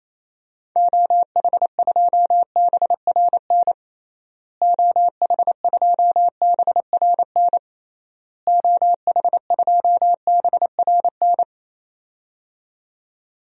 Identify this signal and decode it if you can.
Morse code, heard as a keyed tone, about 28 wpm, 700 Hz. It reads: OH2BRN OH2BRN OH2BRN
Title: DXCC-100 Call Signs - 6 Characters - Repeated 3x - 28wpm